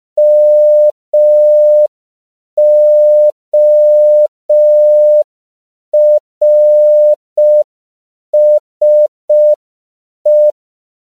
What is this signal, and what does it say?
Morse code, heard as a keyed tone, about 5 wpm, 600 Hz. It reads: MORSE